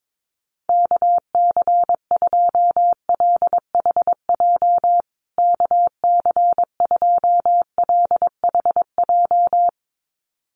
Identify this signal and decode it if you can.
Morse code, heard as a keyed tone, about 22 wpm, 700 Hz. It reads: KC2LHJ KC2LHJ
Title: USA Call Signs Repeated 2x 22wpm